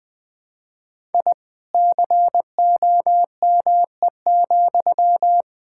Morse code, heard as a keyed tone, about 20 wpm, 700 Hz. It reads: I COME,
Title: Sentences from Top 1000 Words - Review Part 5 - 20wpm